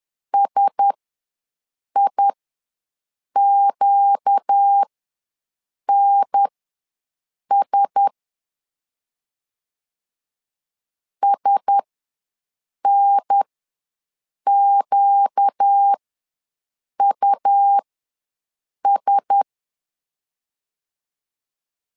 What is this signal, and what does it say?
Morse code, heard as a keyed tone, about 10 wpm, spaced as for 6 wpm, 800 Hz. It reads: SIQNS SNQUS
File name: LEZIONI / 4 Q1S ws 10wpm 2208